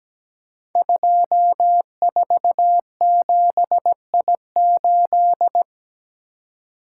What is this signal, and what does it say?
Morse code, heard as a keyed tone, about 17 wpm, 700 Hz. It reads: 247I8
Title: Code Groups - 5 Letter:Numbers - 17wpm